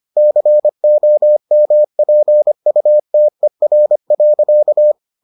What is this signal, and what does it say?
Morse code, heard as a keyed tone, about 25 wpm, 600 Hz. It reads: COMPUTER.